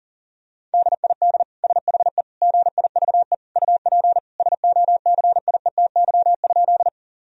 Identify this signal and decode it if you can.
Morse code, heard as a keyed tone, about 40 wpm, 700 Hz. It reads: DID SHE GIVE UP SOCIETY?
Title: Sentences from Top 1000 Words - Review Part 4 - 40wpm